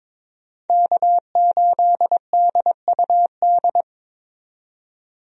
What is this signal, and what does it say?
Morse code, heard as a keyed tone, about 23 wpm, 700 Hz. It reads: K8DUD
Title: USA Call Signs - 5 Characters - 22wpm